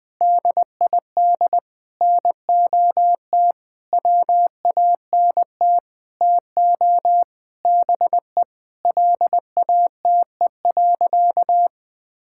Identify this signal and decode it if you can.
Morse code, heard as a keyed tone, about 20 wpm, 700 Hz. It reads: DID NOT WANT TO BE LATE.
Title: Sentences with 6-Word Phrases - 20wpm